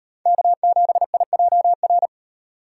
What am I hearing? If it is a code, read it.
Morse code, heard as a keyed tone, about 31 wpm, 700 Hz. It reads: K7IJR
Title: Speed Racing - USA 1x3 Call Signs 25wpm